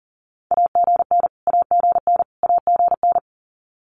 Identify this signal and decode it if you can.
Morse code, heard as a keyed tone, about 40 wpm, 700 Hz. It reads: AGN AGN AGN
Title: Abbreviations Repeated 3x - 40wpm